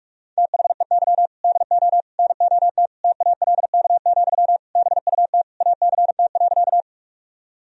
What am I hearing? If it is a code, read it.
Morse code, heard as a keyed tone, about 46 wpm, 700 Hz. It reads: THEY DO NOT TALK, BUT ACT.
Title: Sentences from Top 1000 Words - Review Part 1 - 45wpm